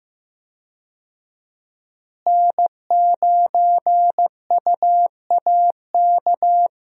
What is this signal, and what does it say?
Morse code, heard as a keyed tone, about 15 wpm, 700 Hz. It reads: N9UAK
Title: USA Call Signs - 5 Characters - Repeated 2x - 15wpm